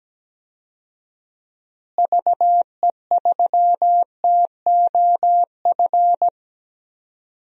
Morse code, heard as a keyed tone, about 17 wpm, 700 Hz. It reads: VE3TOF
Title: DXCC-100 Call Signs - 6 Characters - Repeated 3x - 17wpm